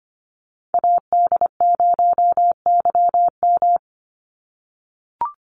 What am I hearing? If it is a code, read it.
Morse code, heard as a keyed tone, about 25 wpm, 700 Hz. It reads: AD0YM
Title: NA SOTA Chaser Callsigns 25wpm